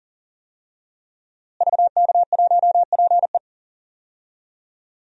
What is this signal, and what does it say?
Morse code, heard as a keyed tone, about 40 wpm, 700 Hz. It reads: VK1PE
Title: DXCC-100 Call Signs - 5 Characters - Repeated 3x - 40wpm